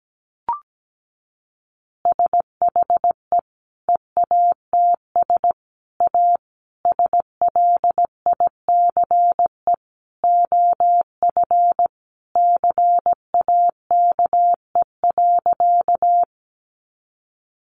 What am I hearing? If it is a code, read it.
Morse code, heard as a keyed tone, about 17 wpm, 700 Hz. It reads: SHE EATS A SLICE OF CAKE.
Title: Easy Phrases 17wpm